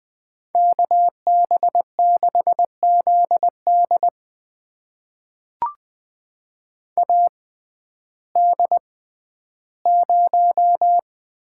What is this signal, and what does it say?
Morse code, heard as a keyed tone, about 21 wpm, spaced as for 12 wpm, 700 Hz. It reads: KB6ZD AD0
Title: USA Call Signs - 5 Characters - 6x Character Spacing - 20wpm